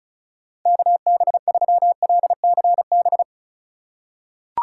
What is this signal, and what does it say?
Morse code, heard as a keyed tone, about 36 wpm, 700 Hz. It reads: KB3LCB
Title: USA Call Signs - 3x Character Spacing - 35wpm